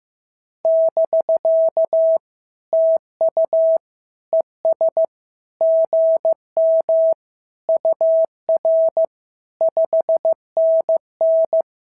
Morse code, heard as a keyed tone, about 15 wpm, 650 Hz. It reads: <BK> TU ES GM UR 5NN
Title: SOTA QSO Practice 15wpm